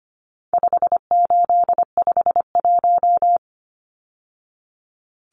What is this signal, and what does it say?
Morse code, heard as a keyed tone, about 25 wpm, 700 Hz. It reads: 5851
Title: Numbers - 4-Digit 25wpm